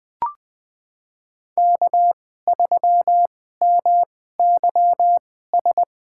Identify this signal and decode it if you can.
Morse code, heard as a keyed tone, about 21 wpm, spaced as for 15 wpm, 700 Hz. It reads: K3MYS
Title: USA Call Signs - 5 Characters - 2x Character Spacing - 20wpm